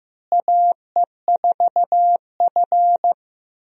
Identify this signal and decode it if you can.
Morse code, heard as a keyed tone, about 15 wpm, 700 Hz. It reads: AE4F
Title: USA Call Signs - 4 Characters - Repeated 3x - 15wpm